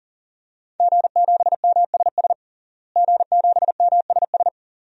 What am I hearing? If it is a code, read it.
Morse code, heard as a keyed tone, about 37 wpm, 700 Hz. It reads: G7MSS G7MSS
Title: DXCC-100 Call Signs - 5 Characters - Repeated 2x - 40wpm